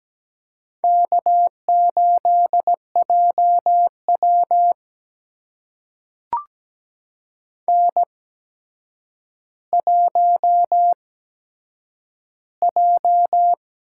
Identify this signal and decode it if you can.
Morse code, heard as a keyed tone, about 17 wpm, spaced as for 8 wpm, 700 Hz. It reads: K8JW N1J
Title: USA Call Signs - 4 Characters - 8x Character Spacing - 17wpm